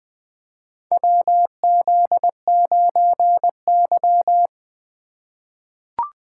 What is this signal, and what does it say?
Morse code, heard as a keyed tone, about 20 wpm, 700 Hz. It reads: WZ9Y
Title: USA Call Signs - 4 Characters - 8x Character Spacing - 20wpm